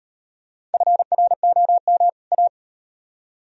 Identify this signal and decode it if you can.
Morse code, heard as a keyed tone, about 38 wpm, 700 Hz. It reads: FROM A